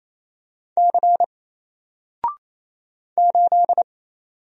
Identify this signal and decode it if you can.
Morse code, heard as a keyed tone, about 27 wpm, 700 Hz. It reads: C 8
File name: Single-Letter-Number-28wpm